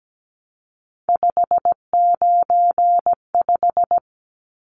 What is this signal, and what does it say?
Morse code, heard as a keyed tone, about 17 wpm, 700 Hz. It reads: 595
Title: Numbers - 3-Digit 17wpm